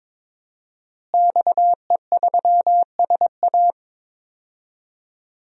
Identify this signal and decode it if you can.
Morse code, heard as a keyed tone, about 22 wpm, 700 Hz. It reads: XE3SA
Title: DXCC-100 Call Signs - 5 Characters - Repeated 2x - 22wpm